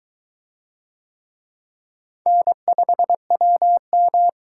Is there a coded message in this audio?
Yes: Morse code from 2.3 to 4.3 s, heard as a keyed tone, about 23 wpm, 700 Hz. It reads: N5WM